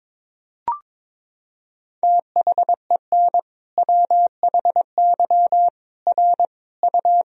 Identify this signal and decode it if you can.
Morse code, heard as a keyed tone, about 22 wpm, 700 Hz. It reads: THEN WHY R U
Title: Sentences from Top 200 Words and Abbreviations 22wpm